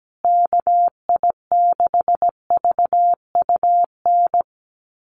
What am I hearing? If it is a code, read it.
Morse code, heard as a keyed tone, about 17 wpm, 700 Hz. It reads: KI6VUN
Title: USA Call Signs Repeated 2x 17wpm